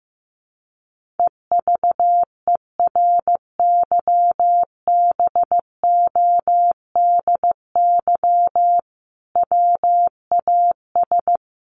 Morse code, heard as a keyed tone, about 15 wpm, 700 Hz. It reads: EVERYBODY WAS
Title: Sentences from Top 1400 Words 15wpm